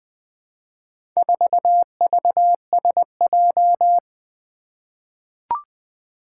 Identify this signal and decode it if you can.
Morse code, heard as a keyed tone, about 20 wpm, 700 Hz. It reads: 4VSJ
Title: Four Challenging Letter-Number Combinations 20wpm